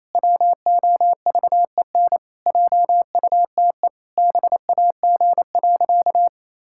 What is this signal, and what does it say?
Morse code, heard as a keyed tone, about 28 wpm, 700 Hz. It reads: WOVEN JUTE BAG.
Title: Pangram Challenge v2 - 28wpm